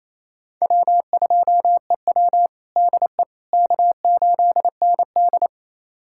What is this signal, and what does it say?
Morse code, heard as a keyed tone, about 28 wpm, 700 Hz. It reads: W2EW DE K8ND
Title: USA Callsign DE Callsign - 4 Characters - 28wpm